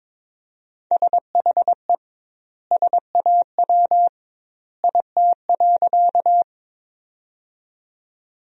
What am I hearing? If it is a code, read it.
Morse code, heard as a keyed tone, about 22 wpm, 700 Hz. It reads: SHE SAW IT.
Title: 2x Word-Spacing - Sentences from Top 300 Words - 22wpm